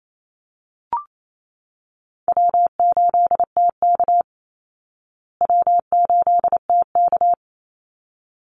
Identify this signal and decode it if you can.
Morse code, heard as a keyed tone, about 28 wpm, 700 Hz. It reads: W8TK W8TK
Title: Easy Call Signs Repeated 2x - 28wpm